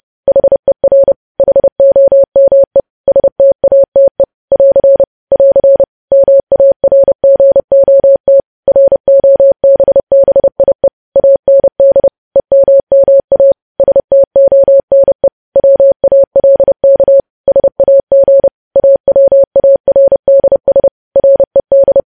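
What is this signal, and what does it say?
Morse code, heard as a keyed tone, about 30 wpm, 550 Hz. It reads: HER HOME STATE <AR> <AR> MARGOT ROBBIE AND EMMA STONE WALK SAG AWARDS RED
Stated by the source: News headlines sent at 30 wpm.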